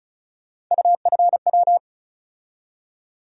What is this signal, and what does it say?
Morse code, heard as a keyed tone, about 36 wpm, 700 Hz. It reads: UFW
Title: USA Call Sign Postfixes - 3 Letters 35wpm